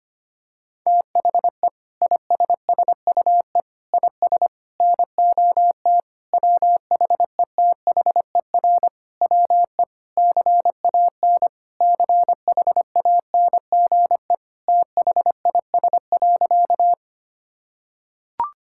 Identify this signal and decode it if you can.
Morse code, heard as a keyed tone, about 25 wpm, 700 Hz. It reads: THE ISSUE IS NOT WHETHER WE CAN CHANGE THIS.
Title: Sentences from Top 1000 Words - Review Part 6 - 25wpm